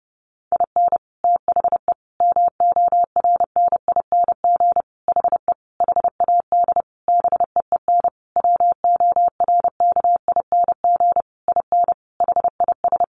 Morse code, heard as a keyed tone, about 30 wpm, 700 Hz. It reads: IN THE MORNING HE HAD BEEN WORKING IN HIS
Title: Sentences from Top 1200 Words 30wpm